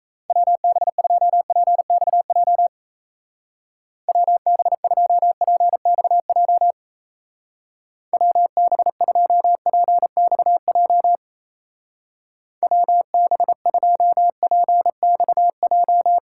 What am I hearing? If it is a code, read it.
Morse code, heard as a keyed tone, about 33 wpm, 700 Hz. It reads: WB2PXJ WB2PXJ WB2PXJ WB2PXJ
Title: Speed Racing - USA 2x3 Call Signs 28wpm